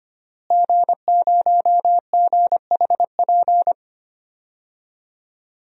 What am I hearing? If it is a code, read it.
Morse code, heard as a keyed tone, about 25 wpm, 700 Hz. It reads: G0GHP
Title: DXCC-100 Call Signs - 5 Characters - Repeated 2x - 25wpm